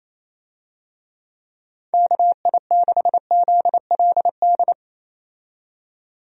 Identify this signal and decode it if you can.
Morse code, heard as a keyed tone, about 28 wpm, 700 Hz. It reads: KI6ZLD
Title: USA Call Signs - 6 Characters - 28wpm